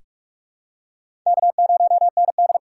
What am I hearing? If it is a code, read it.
Morse code, heard as a keyed tone, about 46 wpm, 690 Hz. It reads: K0ND
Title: USA Call Signs - 4 Characters - 3x Character Spacing - 45wpm